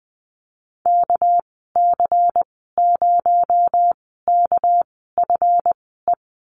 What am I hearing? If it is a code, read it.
Morse code, heard as a keyed tone, about 20 wpm, spaced as for 15 wpm, 700 Hz. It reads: KC0KFE
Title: 231 Lesson 37g - Letter x - Call Signs 20wpm